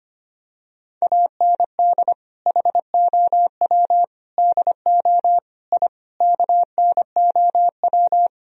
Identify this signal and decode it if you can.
Morse code, heard as a keyed tone, about 25 wpm, 700 Hz. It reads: AND HOW DO I KNOW
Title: Sentences from Top 1000 Words - Review Part 3 - 25wpm